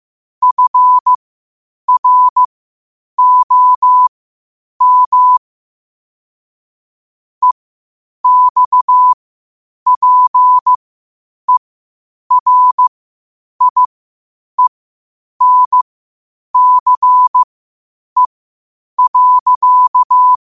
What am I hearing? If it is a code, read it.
Morse code, heard as a keyed tone, about 15 wpm, spaced as for 8 wpm, 1000 Hz. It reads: FROM EXPERIENCE.
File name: qotdfiles / 8wpm